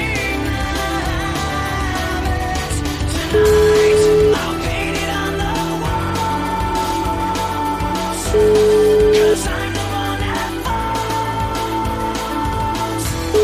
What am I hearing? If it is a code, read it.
Ringback tone (Europe, 425 Hz), with noise.